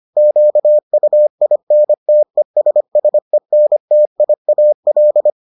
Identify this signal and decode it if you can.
Morse code, heard as a keyed tone, about 25 wpm, 600 Hz. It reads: QUINTESSENTIAL